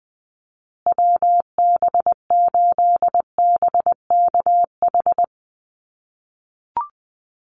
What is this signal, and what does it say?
Morse code, heard as a keyed tone, about 20 wpm, 700 Hz. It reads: WB8BKH
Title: USA Call Signs Repeated 3x 20wpm